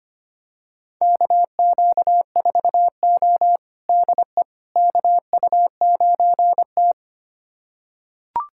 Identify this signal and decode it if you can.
Morse code, heard as a keyed tone, about 25 wpm, 700 Hz. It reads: KQ4O DE KU9T
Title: USA Callsign DE Callsign - 4 Characters - 25wpm